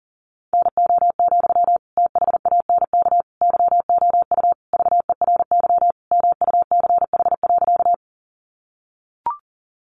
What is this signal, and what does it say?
Morse code, heard as a keyed tone, about 40 wpm, 700 Hz. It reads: NO, THANK YOU VERY MUCH.
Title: Sentences from Top 600 Words 40wpm